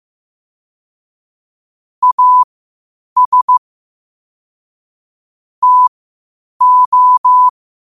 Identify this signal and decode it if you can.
Morse code, heard as a keyed tone, about 15 wpm, spaced as for 8 wpm, 1000 Hz. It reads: AS TO
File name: qotdfiles / 8wpm